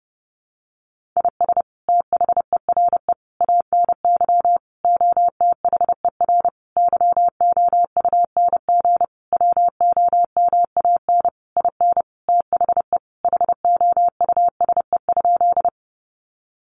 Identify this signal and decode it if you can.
Morse code, heard as a keyed tone, about 30 wpm, 700 Hz. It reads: IS THERE ANY OTHER YOUNG WOMAN IN THE HOUSE?
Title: Sentences from Top 1100 Words 30wpm